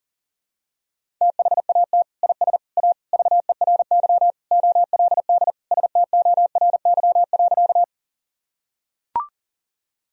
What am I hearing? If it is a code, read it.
Morse code, heard as a keyed tone, about 40 wpm, 700 Hz. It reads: THAT IS A VERY OLD STORY.
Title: Sentences from Top 1000 Words - Review Part 1 - 40wpm